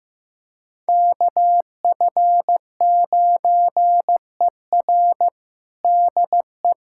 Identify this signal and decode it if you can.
Morse code, heard as a keyed tone, about 15 wpm, 700 Hz. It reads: KF9ER DE
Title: USA Callsign DE Callsign 15wpm